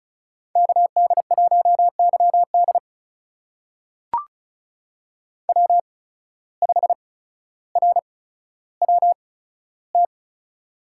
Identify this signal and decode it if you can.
Morse code, heard as a keyed tone, about 36 wpm, spaced as for 15 wpm, 700 Hz. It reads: KD1YD W5RWT
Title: USA Call Signs - 5 Characters - 8x Character Spacing - 35wpm